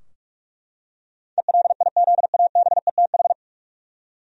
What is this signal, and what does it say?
Morse code, heard as a keyed tone, about 42 wpm, 700 Hz. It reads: ELIZABETH
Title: USA Common Names - 45wpm